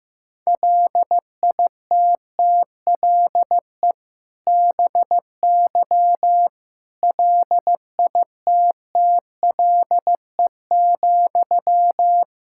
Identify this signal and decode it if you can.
Morse code, heard as a keyed tone, about 16 wpm, 700 Hz. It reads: LITTLE BY LITTLE,
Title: Sentences from Top 1600 Words 15wpm